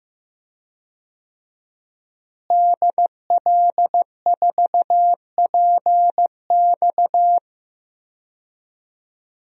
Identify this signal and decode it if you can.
Morse code, heard as a keyed tone, about 15 wpm, 700 Hz. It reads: DL4PX
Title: DXCC-100 Call Signs - 5 Characters - Repeated 3x - 15wpm